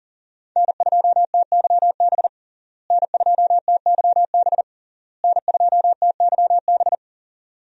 Morse code, heard as a keyed tone, about 40 wpm, 700 Hz. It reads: N2TYB N2TYB N2TYB
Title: USA Call Signs - 5 Characters - Repeated 3x - 40wpm